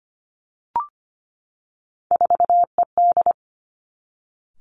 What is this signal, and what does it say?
Morse code, heard as a keyed tone, about 25 wpm, 700 Hz. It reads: 4ED
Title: Three Challenging Letter-Number Combinations 25wpm